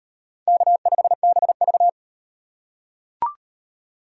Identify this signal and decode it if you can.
Morse code, heard as a keyed tone, about 35 wpm, 700 Hz. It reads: K5BV